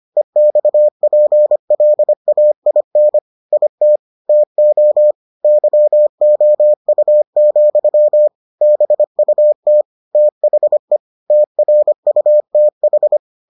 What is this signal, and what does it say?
Morse code, heard as a keyed tone, about 25 wpm, 600 Hz. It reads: EXPLAIN IT TO YOU, BUT THE TRUTH